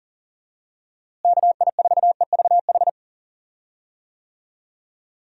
Morse code, heard as a keyed tone, about 39 wpm, 700 Hz. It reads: KI4EVH
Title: USA Call Signs - 6 Characters - Repeated 3x - 40wpm